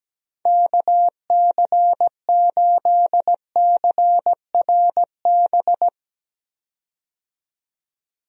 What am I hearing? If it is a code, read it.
Morse code, heard as a keyed tone, about 17 wpm, 700 Hz. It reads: KC8CRB
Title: USA Call Signs - 6 Characters - 17wpm